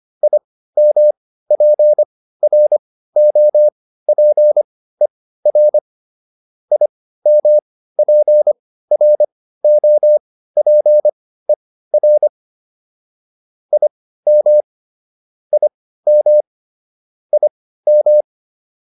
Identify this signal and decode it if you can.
Morse code, heard as a keyed tone, about 25 wpm, spaced as for 15 wpm, 600 Hz. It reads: IMPROPER IMPROPER IM IM IM